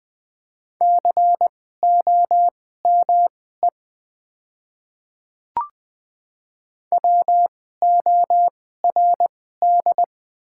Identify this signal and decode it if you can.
Morse code, heard as a keyed tone, about 21 wpm, spaced as for 15 wpm, 700 Hz. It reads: COME WORD
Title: 2x Character Spacing - Top 200 Words - 20wpm